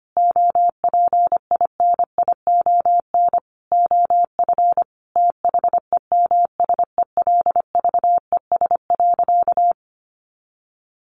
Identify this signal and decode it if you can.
Morse code, heard as a keyed tone, about 25 wpm, 700 Hz. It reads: OPINION OF THEMSELVES.
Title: Sentences from Top 1300 Words 25wpm